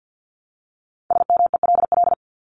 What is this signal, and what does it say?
Morse code, heard as a keyed tone, about 50 wpm, 700 Hz. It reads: SMELL
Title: Top 1501-1750 Words 50wpm